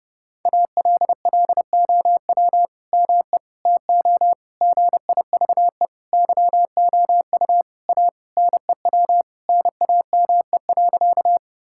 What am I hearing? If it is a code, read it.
Morse code, heard as a keyed tone, about 32 wpm, 700 Hz. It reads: ALLOW ME TO GIVE YOU A NEW NAME.
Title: Sentences from Top 1000 Words - Review Part 5 - 30wpm